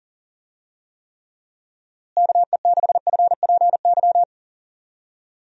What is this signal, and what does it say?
Morse code, heard as a keyed tone, about 40 wpm, 700 Hz. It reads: KE6FPY
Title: USA Call Signs - 6 Characters - Repeated 3x - 40wpm